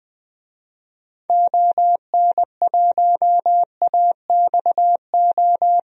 Morse code, heard as a keyed tone, about 21 wpm, 700 Hz. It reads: ON1AXO
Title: DXCC-100 Call Signs - 6 Characters - 20wpm